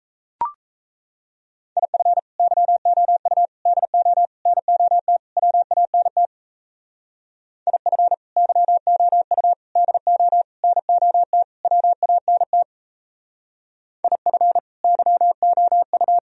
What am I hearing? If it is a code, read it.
Morse code, heard as a keyed tone, about 35 wpm, 700 Hz. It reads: IF YOU DO NOT WANT IF YOU DO NOT WANT IF YOU
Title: Speed-Racing - Sets of 5 Words 28wpm